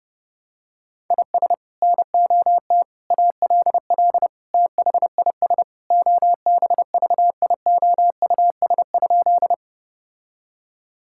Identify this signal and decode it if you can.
Morse code, heard as a keyed tone, about 30 wpm, 700 Hz. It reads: IS NOT ALL THIS OBVIOUS?
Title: Sentences from Top 1600 Words 30wpm